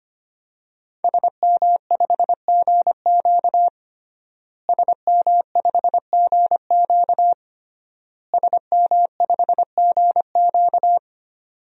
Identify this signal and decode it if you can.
Morse code, heard as a keyed tone, about 25 wpm, 700 Hz. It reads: SM5GQ SM5GQ SM5GQ
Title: DXCC-100 Call Signs - 5 Characters - Repeated 3x - 25wpm